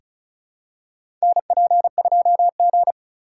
Morse code, heard as a keyed tone, about 36 wpm, 700 Hz. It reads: NP2G
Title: USA Call Signs - 4 Characters - 3x Character Spacing - 35wpm